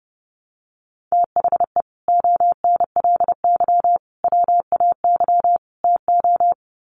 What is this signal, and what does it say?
Morse code, heard as a keyed tone, about 30 wpm, 700 Hz. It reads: THE ONLY WAY TO
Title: Sets of 4 Words 30wpm